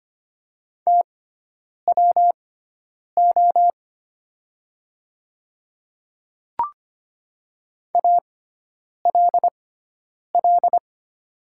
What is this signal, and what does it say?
Morse code, heard as a keyed tone, about 25 wpm, spaced as for 9 wpm, 700 Hz. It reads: TWO ALL
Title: 6x Character Spacing - Top 300 Words - 25wpm